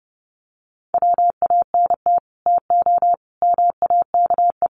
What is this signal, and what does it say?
Morse code, heard as a keyed tone, about 28 wpm, 700 Hz. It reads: WANT TO MAKE
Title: Sets of 5 Words 30wpm